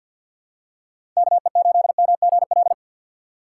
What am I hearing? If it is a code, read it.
Morse code, heard as a keyed tone, about 53 wpm, 700 Hz. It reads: KE8MGL
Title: USA Call Signs - 2x Character Spacing - 50wpm